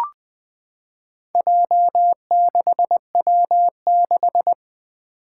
Morse code, heard as a keyed tone, about 20 wpm, 700 Hz. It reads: J6W6
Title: Four Challenging Letter-Number Combinations 20wpm